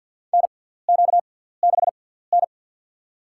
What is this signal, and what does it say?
Morse code, heard as a keyed tone, about 50 wpm, spaced as for 17 wpm, 700 Hz. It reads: NQ6N
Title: USA Call Signs - 4 Characters - 6x Character Spacing - 50wpm